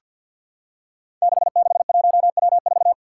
Morse code, heard as a keyed tone, about 51 wpm, 700 Hz. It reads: 661W4
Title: Five Challenging Letter-Number Combinations 50wpm